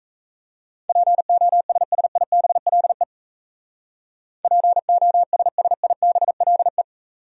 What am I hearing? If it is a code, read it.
Morse code, heard as a keyed tone, about 35 wpm, 700 Hz. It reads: POSSIBLE POSSIBLE
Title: Speed-Racing Top 500 Words 28wpm